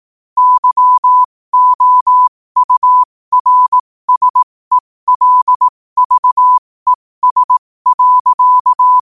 Morse code, heard as a keyed tone, about 18 wpm, spaced as for 15 wpm, 1000 Hz. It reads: YOURSELVES.
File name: qotdfiles / 15wpm